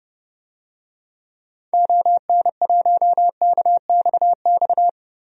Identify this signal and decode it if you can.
Morse code, heard as a keyed tone, about 30 wpm, 700 Hz. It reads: ON1KXX
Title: DXCC-100 Call Signs - 6 Characters - 30wpm